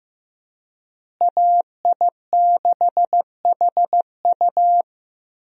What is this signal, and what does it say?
Morse code, heard as a keyed tone, about 15 wpm, 700 Hz. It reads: AI6HU
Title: USA Call Signs - 5 Characters - Repeated 2x - 15wpm